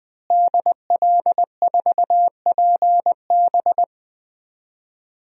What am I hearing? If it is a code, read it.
Morse code, heard as a keyed tone, about 20 wpm, 700 Hz. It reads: DL4PB
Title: DXCC-100 Call Signs - 5 Characters - Repeated 2x - 20wpm